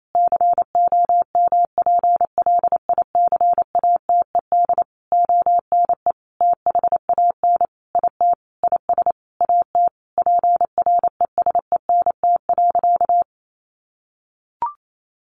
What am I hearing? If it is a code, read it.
Morse code, heard as a keyed tone, about 28 wpm, 700 Hz. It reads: COMPLICATED ONE THAN IT IS AT PRESENT.
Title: Sentences from Top 1200 Words 28wpm